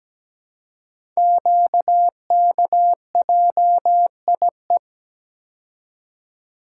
Morse code, heard as a keyed tone, about 17 wpm, 700 Hz. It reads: QKJIE
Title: Code Groups - 5 Letters - 17wpm